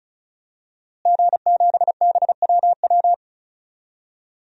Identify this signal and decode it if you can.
Morse code, heard as a keyed tone, about 36 wpm, 700 Hz. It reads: G7BWW
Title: DXCC-100 Call Signs - 5 Characters - Repeated 2x - 35wpm